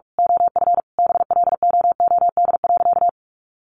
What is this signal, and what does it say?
Morse code, heard as a keyed tone, about 44 wpm, 700 Hz. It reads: OF BLOOD.
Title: A Sound Of Thunder 45wpm